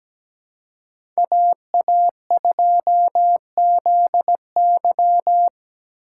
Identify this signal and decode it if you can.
Morse code, heard as a keyed tone, about 17 wpm, 700 Hz. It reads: AA2ZY
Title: USA Call Signs - 5 Characters - 4x Character Spacing - 17wpm